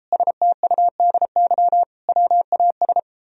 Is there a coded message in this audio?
Yes: Morse code from 0.1 to 3.0 s, heard as a keyed tone, about 33 wpm, 700 Hz. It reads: STUDY WAS